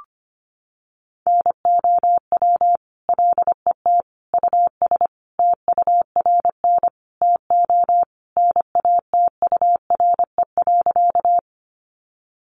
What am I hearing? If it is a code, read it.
Morse code, heard as a keyed tone, about 25 wpm, 700 Hz. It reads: NOW LET US TURN TO NATURE.
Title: Sentences from Top 600 Words 25wpm